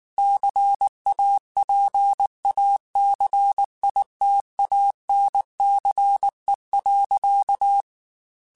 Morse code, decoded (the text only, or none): CAPACITANCE.